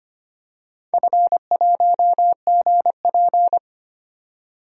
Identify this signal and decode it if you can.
Morse code, heard as a keyed tone, about 25 wpm, 700 Hz. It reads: F1GP
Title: DXCC-100 Call Signs - 4 Characters - 25wpm